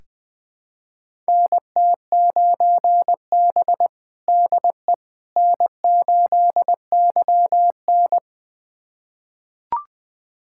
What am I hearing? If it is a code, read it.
Morse code, heard as a keyed tone, about 21 wpm, 700 Hz. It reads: NT9B DE N8YN
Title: USA Callsign DE Callsign - 4 Characters - 20wpm